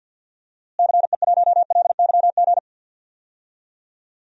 Morse code, heard as a keyed tone, about 45 wpm, 700 Hz. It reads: CE1LYG
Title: DXCC-100 Call Signs - 6 Characters - Repeated 3x - 50wpm